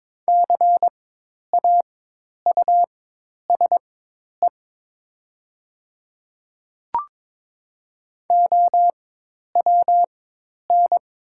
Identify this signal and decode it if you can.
Morse code, heard as a keyed tone, about 22 wpm, spaced as for 10 wpm, 700 Hz. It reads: CAUSE OWN
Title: 4x Character Spacing - Top 300 Words - 22wpm